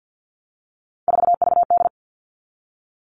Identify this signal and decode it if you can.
Morse code, heard as a keyed tone, about 48 wpm, 700 Hz. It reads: 4VD
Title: All Cut Numbers - 3-Digit 50wpm